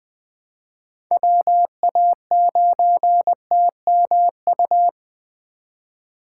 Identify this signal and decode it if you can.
Morse code, heard as a keyed tone, about 21 wpm, 700 Hz. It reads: WA9TMU
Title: 2023 POTA Hunter Call Signs - 20wpm